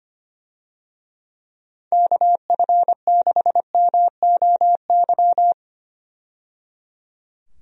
Morse code, heard as a keyed tone, about 25 wpm, 700 Hz. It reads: KF6MOY
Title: USA Call Signs - 6 Characters - 25wpm